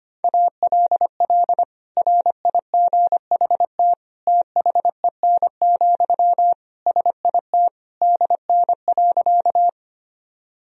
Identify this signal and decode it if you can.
Morse code, heard as a keyed tone, about 25 wpm, 700 Hz. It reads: ALL RIGHT THEN, SIT DN.
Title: Sentences from Top 500 Words and Abbreviations 25wpm